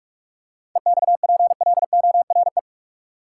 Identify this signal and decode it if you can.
Morse code, heard as a keyed tone, about 45 wpm, 700 Hz. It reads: EXPLORE
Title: Seven Letter Words - 45wpm